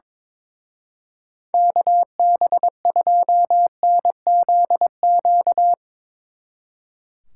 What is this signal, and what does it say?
Morse code, heard as a keyed tone, about 22 wpm, 700 Hz. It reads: KB2NZQ
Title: Speed Racing - USA 2x3 Call Signs 22wpm